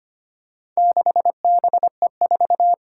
Morse code, heard as a keyed tone, about 25 wpm, 700 Hz. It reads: 6BE4
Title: Four Challenging Letter-Number Combinations 25wpm